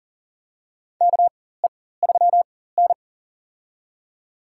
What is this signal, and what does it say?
Morse code, heard as a keyed tone, about 40 wpm, spaced as for 19 wpm, 700 Hz. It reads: KE3N
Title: USA Call Signs - 4 Characters - 4x Character Spacing - 40wpm